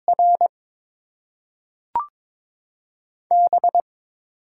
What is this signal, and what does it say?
Morse code, heard as a keyed tone, about 21 wpm, 700 Hz. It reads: R B
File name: Top 20 QSO Elements - 8x Character Spacing - 22wpm